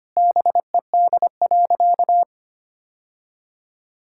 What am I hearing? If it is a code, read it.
Morse code, heard as a keyed tone, about 25 wpm, 700 Hz. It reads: BED.
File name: practice / 25wpm / Sentences with 3-Word Phrases - 25wpm